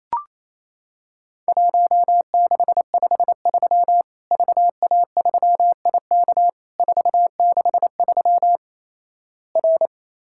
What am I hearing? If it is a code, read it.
Morse code, heard as a keyed tone, about 28 wpm, 700 Hz. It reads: 1653 VA3IK 463 R
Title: Worked All Europe DX Contest - QTC Exchange - 28wpm